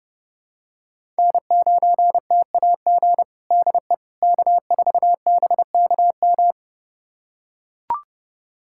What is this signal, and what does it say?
Morse code, heard as a keyed tone, about 32 wpm, 700 Hz. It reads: N9TAG DE K4BKM
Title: USA Callsign DE Callsign - 5 Characters - 30wpm